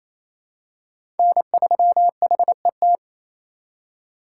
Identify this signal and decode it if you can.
Morse code, heard as a keyed tone, about 28 wpm, 700 Hz. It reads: N3HET
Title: USA Call Signs - 5 Characters - 2x Character Spacing - 28wpm